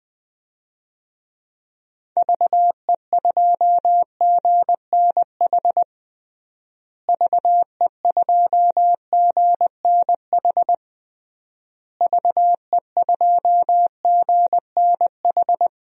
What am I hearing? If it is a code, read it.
Morse code, heard as a keyed tone, about 20 wpm, 700 Hz. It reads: VE2GNH VE2GNH VE2GNH
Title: DXCC-100 Call Signs - 6 Characters - Repeated 3x - 20wpm